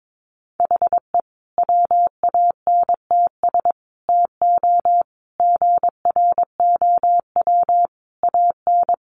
Morse code, heard as a keyed tone, about 22 wpm, 700 Hz. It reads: HE WANTS TO GROW AN
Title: Easy Phrases 22wpm